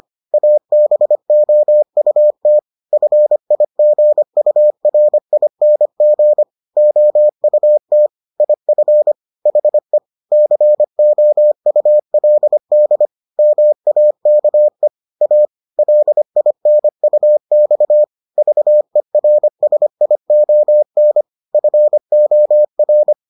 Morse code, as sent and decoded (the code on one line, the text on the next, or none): .- -... --- ..- - / ..-. .. --. ..- .-. .. -. --. / --- ..- - / .. ..-. / .... / -.-. --- ..- .-.. -.. / -- .- -.- . / .- / .-.. .. -. ..- -..- / ...- . .-. ... .. --- -. / ..-. --- .-.
ABOUT FIGURING OUT IF HE COULD MAKE A LINUX VERSION FOR